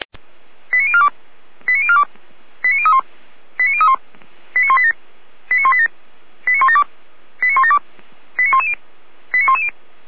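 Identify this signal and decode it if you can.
Five-tone selective call (ZVEI), heard as quick runs of five tones, tones of about 70 ms. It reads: 89042 89042 89031 89031 89187 89187 89172 89172 89110 89110
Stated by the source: ZVEI five-tone selective call received by SDR, fire brigade Orlamünde (feuerwehr-orlamuende).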